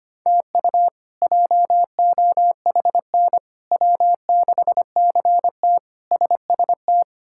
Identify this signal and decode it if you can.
Morse code, heard as a keyed tone, about 25 wpm, 700 Hz. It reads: TU JOHN W6CT SST
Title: K1USN SST QSO Practice - 25wpm